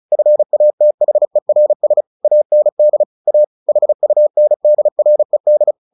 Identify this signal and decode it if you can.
Morse code, heard as a keyed tone, about 34 wpm, 600 Hz. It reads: FATHERS AND A HUNDRED
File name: the old man and the sea 2 35wpm